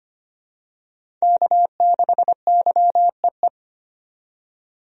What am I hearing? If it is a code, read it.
Morse code, heard as a keyed tone, about 25 wpm, 690 Hz. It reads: K6YEE
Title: USA Call Signs - 5 Characters - 25wpm